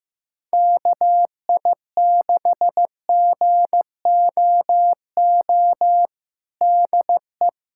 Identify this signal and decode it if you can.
Morse code, heard as a keyed tone, about 16 wpm, 700 Hz. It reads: KI6GOO DE
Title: USA Callsign DE Callsign - 6 Characters - 15wpm